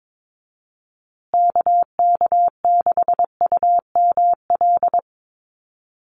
Morse code, heard as a keyed tone, about 22 wpm, 700 Hz. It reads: KK6UML
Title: USA Call Signs Repeated 3x 22wpm